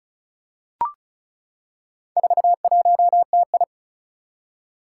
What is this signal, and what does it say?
Morse code, heard as a keyed tone, about 36 wpm, 700 Hz. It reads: 41TI
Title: Four Challenging Letter-Number Combinations 35wpm